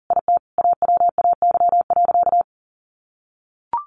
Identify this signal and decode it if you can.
Morse code, heard as a keyed tone, about 40 wpm, 700 Hz. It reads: IT AWAY.
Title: Sentences from Top 600 Words 40wpm